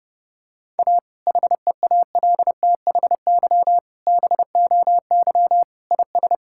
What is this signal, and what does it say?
Morse code, heard as a keyed tone, about 30 wpm, 700 Hz. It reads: A HEALTHY BOY IS
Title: Sentences from Top 1000 Words - Review Part 5 - 30wpm